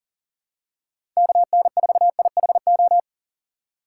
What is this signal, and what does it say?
Morse code, heard as a keyed tone, about 38 wpm, 700 Hz. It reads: KN4IHO
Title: USA Call Signs - 6 Characters - 2x Character Spacing - 40wpm